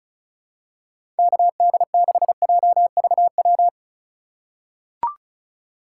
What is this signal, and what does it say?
Morse code, heard as a keyed tone, about 34 wpm, 690 Hz. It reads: KD6JVW
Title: USA Call Signs - 4x Character Spacing - 35wpm